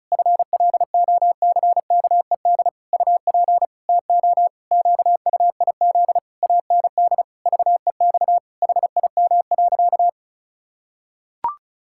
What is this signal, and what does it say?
Morse code, heard as a keyed tone, about 34 wpm, 700 Hz. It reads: FLOCKED UP TO QUIZ AND VEX HIM.
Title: Pangram Challenge v2 - 35wpm